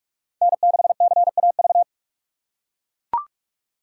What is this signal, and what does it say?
Morse code, heard as a keyed tone, about 44 wpm, 700 Hz. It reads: N6CAV